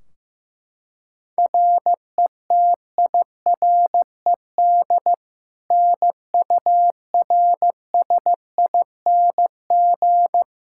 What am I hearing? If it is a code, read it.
Morse code, heard as a keyed tone, about 16 wpm, 700 Hz. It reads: RETIRED NURSING
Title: Practice QSO - Exchange 3 - More Info - 15wpm